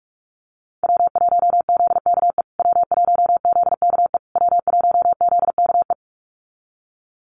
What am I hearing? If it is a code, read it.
Morse code, heard as a keyed tone, about 44 wpm, 690 Hz. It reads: W1ZKE W1ZKE W1ZKE
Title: USA Call Signs Repeated 3x 45wpm